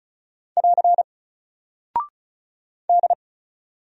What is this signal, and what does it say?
Morse code, heard as a keyed tone, about 36 wpm, 700 Hz. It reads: <AR> D
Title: Single Character (Letter:Number:Punctuation:Prosign) Alt 35wpm